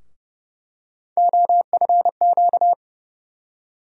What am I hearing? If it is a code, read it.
Morse code, heard as a keyed tone, about 31 wpm, 700 Hz. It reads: OFQ
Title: USA Call Sign Postfixes - 3 Letters 30wpm